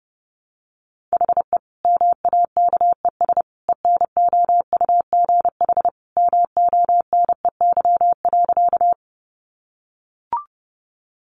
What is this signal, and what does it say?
Morse code, heard as a keyed tone, about 30 wpm, 700 Hz. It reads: HE MAKES ENOUGH MONEY.
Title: Additional Easy Phrases 30wpm